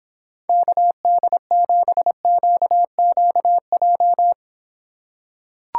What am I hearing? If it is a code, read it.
Morse code, heard as a keyed tone, about 26 wpm, 700 Hz. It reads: KD7QQJ